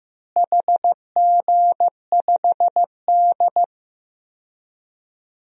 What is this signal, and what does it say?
Morse code, heard as a keyed tone, about 15 wpm, 700 Hz. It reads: HG5D
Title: DXCC-100 Call Signs - 4 Characters - Repeated 3x - 15wpm